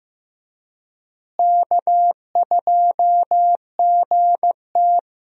Morse code, heard as a keyed tone, about 16 wpm, 700 Hz. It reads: K2GT
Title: USA Call Signs - 4 Characters - Repeated 3x - 15wpm